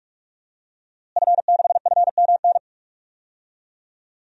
Morse code, heard as a keyed tone, about 45 wpm, 700 Hz. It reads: F6FMN
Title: DXCC-100 Call Signs - 5 Characters - Repeated 3x - 45wpm